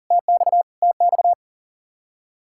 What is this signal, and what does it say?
Morse code, heard as a keyed tone, about 35 wpm, 700 Hz. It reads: TX TX
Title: Speed-Racing Common QSO Phrases Encore - 40wpm